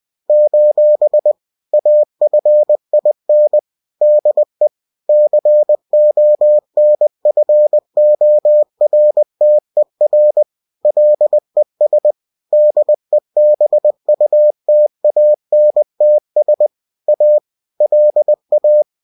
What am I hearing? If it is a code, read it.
Morse code, heard as a keyed tone, about 20 wpm, 600 Hz. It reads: : AFIN DE CONFORTER LES DEBUTANTS A LA